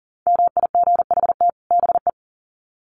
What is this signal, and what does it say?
Morse code, heard as a keyed tone, about 40 wpm, 700 Hz. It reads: MIGHT BE
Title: Sets of 2 Words 40wpm